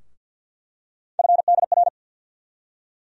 Morse code, heard as a keyed tone, about 51 wpm, 700 Hz. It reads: FDR
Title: USA Call Sign Postfixes - 3 Letters 50wpm